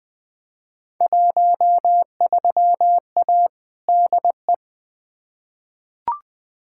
Morse code, heard as a keyed tone, about 20 wpm, 700 Hz. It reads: 13A DE
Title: Field Day Exchanges Encore - 20wpm